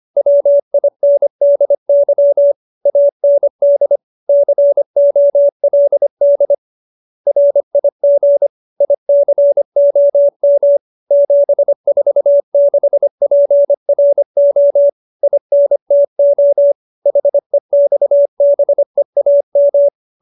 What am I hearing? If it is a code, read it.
Morse code, heard as a keyed tone, about 25 wpm, 570 Hz. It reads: WINDY AND COLD RIG ICOM 746PRO INTO HEXBEAM